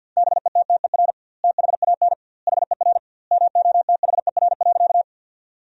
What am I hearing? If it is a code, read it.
Morse code, heard as a keyed tone, about 50 wpm, 700 Hz. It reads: BETTER THAN HER MOTHER.
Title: 2x Word-Spacing - Sentences from Top 400 Words - 50wpm